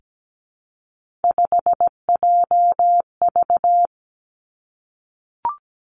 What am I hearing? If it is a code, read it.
Morse code, heard as a keyed tone, about 17 wpm, 700 Hz. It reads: 5JV
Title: Three Challenging Letter-Number Combinations 17wpm